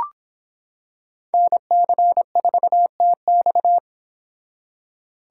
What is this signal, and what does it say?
Morse code, heard as a keyed tone, about 25 wpm, 700 Hz. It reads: NC4TX